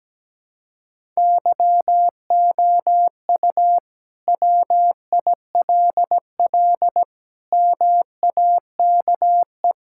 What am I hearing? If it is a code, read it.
Morse code, heard as a keyed tone, about 17 wpm, 700 Hz. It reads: YOU WILL MAKE
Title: Sentences from Top 1000 Words - Review Part 5 - 17wpm